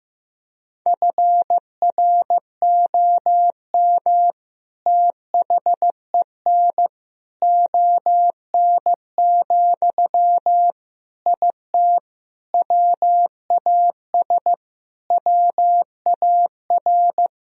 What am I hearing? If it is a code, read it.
Morse code, heard as a keyed tone, about 15 wpm, 700 Hz. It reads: FROM THEN ON, IT WAS WAR
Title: Sentences from Top 1000 Words - Review Part 2 - 15wpm